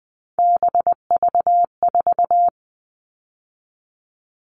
Morse code, heard as a keyed tone, about 20 wpm, 700 Hz. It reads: BV4
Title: All Cut Numbers - 3-Digit 20wpm